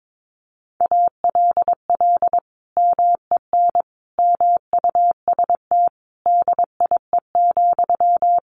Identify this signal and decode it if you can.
Morse code, heard as a keyed tone, about 22 wpm, 700 Hz. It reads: ALL MEN MUST DIE,
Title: Sentences from Top 1300 Words 22wpm